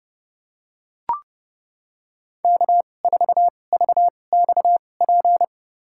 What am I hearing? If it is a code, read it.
Morse code, heard as a keyed tone, about 30 wpm, spaced as for 22 wpm, 700 Hz. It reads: K4VXP
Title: USA Call Signs - 2x Character Spacing - 30wpm